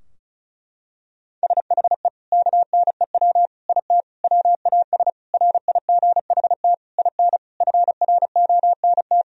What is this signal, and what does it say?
Morse code, heard as a keyed tone, about 36 wpm, 700 Hz. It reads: SHE KNEW IT WAS RIGHT IN FRONT
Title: Sentences from Top 1000 Words - Review Part 2 - 35wpm